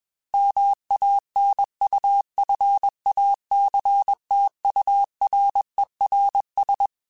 Morse code, heard as a keyed tone, about 21 wpm, 800 Hz. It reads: MANUFACTURERS